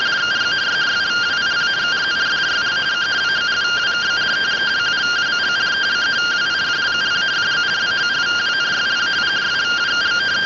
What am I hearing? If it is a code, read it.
Two-tone teleprinter signal, about 170 Hz shift.